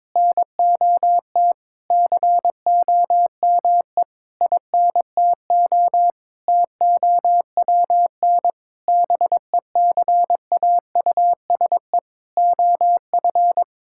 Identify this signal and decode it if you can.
Morse code, heard as a keyed tone, about 21 wpm, 700 Hz. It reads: NOT COME INTO TOWN BECAUSE OF
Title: Sentences from Top 1000 Words - Review Part 5 - 22wpm